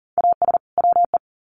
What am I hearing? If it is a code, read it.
Morse code, heard as a keyed tone, about 40 wpm, 700 Hz. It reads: AS WE
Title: Sets of 2 Words 40wpm